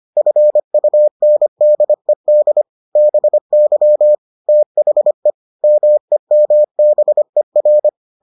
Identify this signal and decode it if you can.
Morse code, heard as a keyed tone, about 25 wpm, 600 Hz. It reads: FUNDED BY THE MEMBER